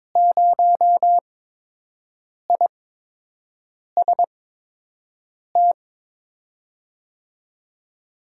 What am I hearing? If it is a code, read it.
Morse code, heard as a keyed tone, about 22 wpm, spaced as for 6 wpm, 700 Hz. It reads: 0IST
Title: USA Call Signs - 5 Characters - 8x Character Spacing - 22wpm